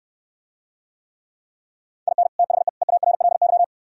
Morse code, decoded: IN THE EARLY

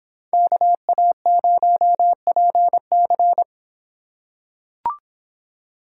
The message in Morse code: KA0PC

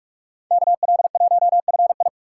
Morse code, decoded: KL1FI

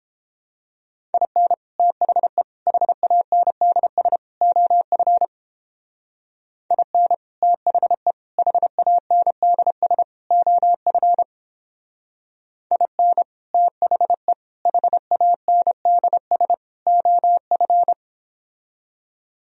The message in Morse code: IN THE HANDS OF IN THE HANDS OF IN THE HANDS OF

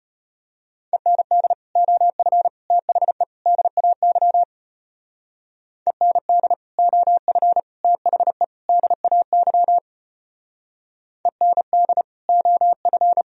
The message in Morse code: END OF THE DAY END OF THE DAY END OF

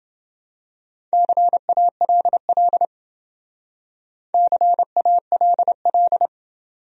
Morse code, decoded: CALL CALL

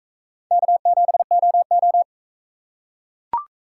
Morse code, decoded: K7OO

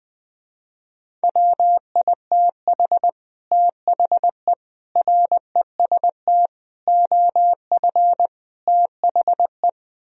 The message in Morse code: WITH THE REST OF THE

